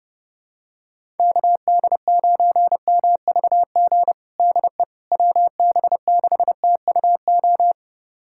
Morse code, heard as a keyed tone, about 30 wpm, 700 Hz. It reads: KD9MVG DE WB6TUO
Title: USA Callsign DE Callsign 30wpm